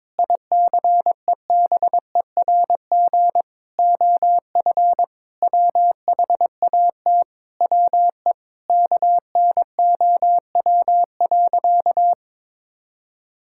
Morse code, heard as a keyed tone, about 22 wpm, 700 Hz. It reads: ICEBERG OF WHAT WE KNOW.
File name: practice / 22wpm / Sentences with 8-Word Phrases - 22wpm